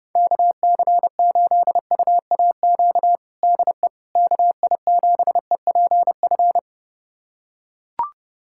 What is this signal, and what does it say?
Morse code, heard as a keyed tone, about 30 wpm, 690 Hz. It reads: KC8UAQ DE KI7EPF
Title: USA Callsign DE Callsign - 6 Characters - 30wpm